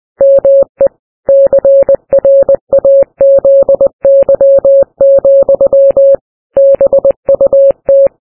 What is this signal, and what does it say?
Morse code, heard as a keyed tone, about 20 wpm, 550 Hz. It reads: ME CRAZY, BUT
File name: morse-ship